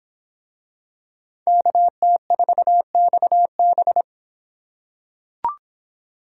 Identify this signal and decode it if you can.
Morse code, heard as a keyed tone, about 25 wpm, 700 Hz. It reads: KT4XB